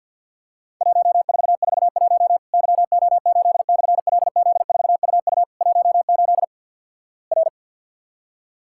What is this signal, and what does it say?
Morse code, heard as a keyed tone, about 50 wpm, 700 Hz. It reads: 1441 YO8/LZ4UU 18 R
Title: Worked All Europe DX Contest - QTC Exchange - 50wpm